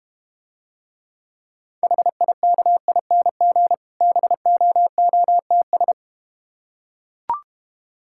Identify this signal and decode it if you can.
Morse code, heard as a keyed tone, about 32 wpm, 700 Hz. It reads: HIKING BOOTS